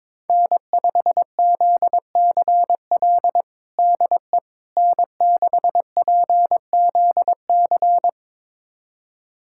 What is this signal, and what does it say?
Morse code, heard as a keyed tone, about 22 wpm, 700 Hz. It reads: N5ZCL DE N6PZC
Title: USA Callsign DE Callsign 22wpm